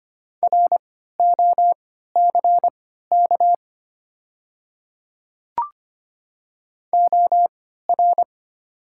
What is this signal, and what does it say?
Morse code, heard as a keyed tone, about 25 wpm, spaced as for 14 wpm, 690 Hz. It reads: ROCK OR
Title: 3x Character Spacing - Top 500 Words - 25wpm